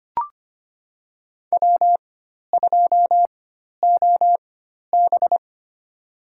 Morse code, heard as a keyed tone, about 25 wpm, spaced as for 12 wpm, 700 Hz. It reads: W2OB
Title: USA Call Signs - 4 Characters - 4x Character Spacing - 25wpm